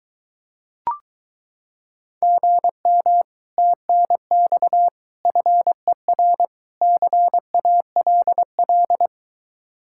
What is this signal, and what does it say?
Morse code, heard as a keyed tone, about 23 wpm, 700 Hz. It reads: GM TNX FER CALL